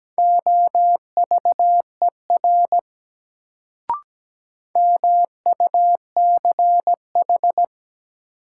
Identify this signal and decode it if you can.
Morse code, heard as a keyed tone, about 17 wpm, 700 Hz. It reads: OVER MUCH
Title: Sets of 2 Words - New Word Drill - 17wpm